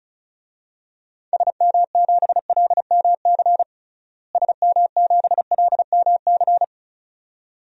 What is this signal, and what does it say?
Morse code, heard as a keyed tone, about 34 wpm, 700 Hz. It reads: SM7LMC SM7LMC
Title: DXCC-100 Call Signs - 6 Characters - Repeated 2x - 35wpm